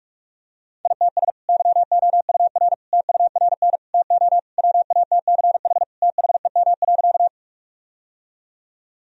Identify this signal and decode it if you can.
Morse code, heard as a keyed tone, about 45 wpm, 700 Hz. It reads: ITS YOUR TURN TO WATCH THEM.
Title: Sentences from Top 1000 Words - Review Part 1 - 45wpm